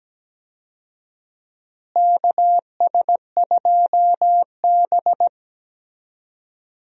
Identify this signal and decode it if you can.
Morse code, heard as a keyed tone, about 17 wpm, 700 Hz. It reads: KS2B
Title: USA Call Signs - 4 Characters - Repeated 3x - 17wpm